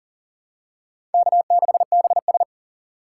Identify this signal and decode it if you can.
Morse code, heard as a keyed tone, about 40 wpm, 700 Hz. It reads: K6BS
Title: USA Call Signs - 4 Characters - 6x Character Spacing - 40wpm